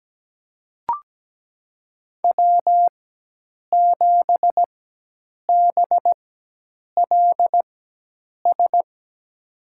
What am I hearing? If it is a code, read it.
Morse code, heard as a keyed tone, about 17 wpm, spaced as for 8 wpm, 700 Hz. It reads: W7BLS
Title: USA Call Signs - 5 Characters - 4x Character Spacing - 17wpm